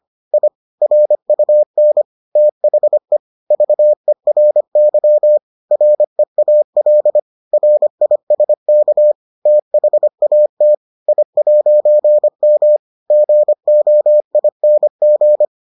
Morse code, heard as a keyed tone, about 25 wpm, 600 Hz. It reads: I RUN THE VERY REAL RISK THAT I'M GOING